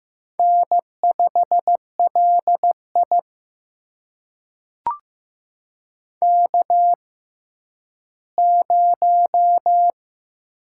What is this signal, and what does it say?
Morse code, heard as a keyed tone, about 15 wpm, spaced as for 10 wpm, 700 Hz. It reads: N5LI K0
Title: USA Call Signs - 6x Character Spacing - 15wpm